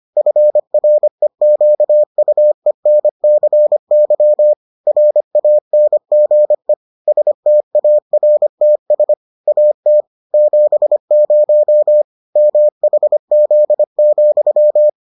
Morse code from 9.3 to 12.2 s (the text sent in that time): AT 70